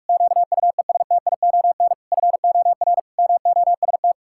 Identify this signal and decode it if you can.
Morse code, heard as a keyed tone, about 48 wpm, 700 Hz. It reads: QUESTION FOR MOST
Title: Pangram Challenge v2 - 45wpm